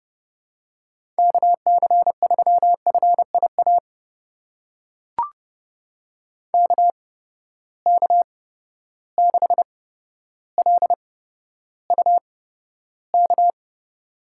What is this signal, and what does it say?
Morse code, heard as a keyed tone, about 30 wpm, spaced as for 13 wpm, 700 Hz. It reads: KC3FIA KK6LUK
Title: USA Call Signs - 6 Characters - 8x Character Spacing - 30wpm